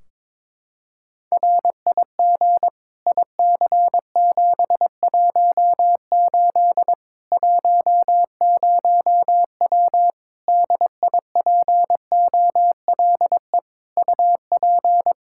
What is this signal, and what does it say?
Morse code, heard as a keyed tone, about 22 wpm, 700 Hz. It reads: RIG IC718 10W DIPOLE UP
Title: Practice QSO - Exchange 2 - Info - 22wpm